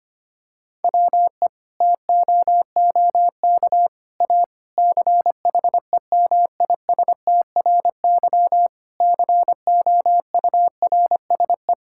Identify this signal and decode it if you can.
Morse code, heard as a keyed tone, about 25 wpm, 700 Hz. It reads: WE TOOK A CHEMISTRY COURSE
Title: Sentences from Top 1000 Words - Review Part 6 - 25wpm